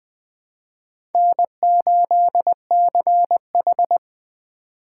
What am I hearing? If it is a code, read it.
Morse code, heard as a keyed tone, about 20 wpm, 700 Hz. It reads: N8CH
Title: USA Call Signs - 3x Character Spacing - 20wpm